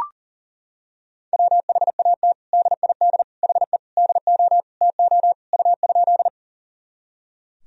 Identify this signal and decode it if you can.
Morse code, heard as a keyed tone, about 37 wpm, 700 Hz. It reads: WHAT DID HE DO TO U?
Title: Sentences from Top 100 Words and Abbreviations 40wpm